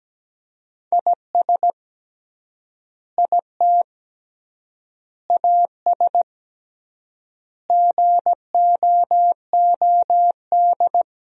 Morse code, heard as a keyed tone, about 17 wpm, 700 Hz. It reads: IS IT AS GOOD
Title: 3x Word-Spacing - Sentences from Top 200 Words - 17wpm